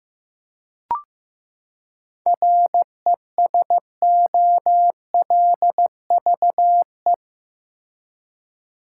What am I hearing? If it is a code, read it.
Morse code, heard as a keyed tone, about 15 wpm, 700 Hz. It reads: RESOLVE
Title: Seven Letter Words - 15wpm